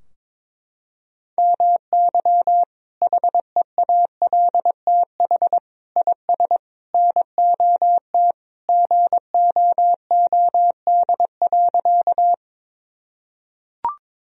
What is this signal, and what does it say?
Morse code, heard as a keyed tone, about 23 wpm, 700 Hz. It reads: MY HEALTH IS NOT GOOD.
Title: Sentences from Top 1000 Words - Review Part 3 - 22wpm